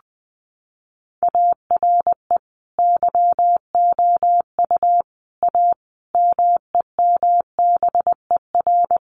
Morse code, heard as a keyed tone, about 20 wpm, 700 Hz. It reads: ARE YOU A MEMBER
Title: Sentences from Top 1400 Words 20wpm